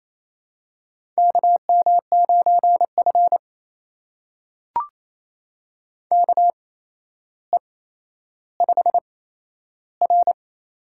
Morse code, heard as a keyed tone, about 28 wpm, spaced as for 12 wpm, 700 Hz. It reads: KM9F KE5R
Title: USA Call Signs - 8x Character Spacing - 28wpm